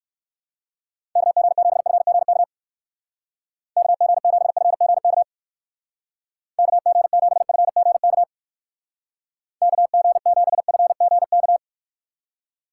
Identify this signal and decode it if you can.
Morse code, heard as a keyed tone, about 58 wpm, 700 Hz. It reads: KG7FGK KG7FGK KG7FGK KG7FGK